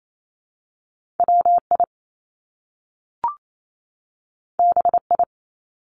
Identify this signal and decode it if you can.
Morse code, heard as a keyed tone, about 28 wpm, 700 Hz. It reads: W I B I
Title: Top Two Letter Combinations in Words 28wpm